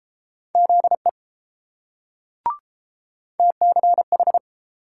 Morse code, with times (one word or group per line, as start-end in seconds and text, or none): 0.5-1.1 ZE
3.4-4.4 TCH